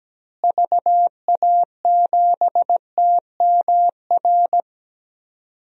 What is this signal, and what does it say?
Morse code, heard as a keyed tone, about 17 wpm, 700 Hz. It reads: VA7TMR
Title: DXCC-100 Call Signs - 6 Characters - Repeated 3x - 17wpm